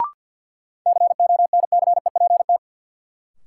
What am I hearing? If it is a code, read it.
Morse code, heard as a keyed tone, about 52 wpm, 700 Hz. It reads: CONCEPT
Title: Sentences from Top 1100 Words - New Word Drill Alt - 50wpm